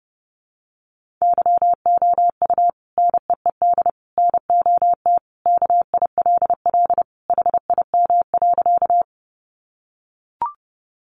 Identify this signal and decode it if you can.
Morse code, heard as a keyed tone, about 28 wpm, 700 Hz. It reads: YOU NEED NOT KILL HIM.
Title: Sentences from Top 1400 Words 30wpm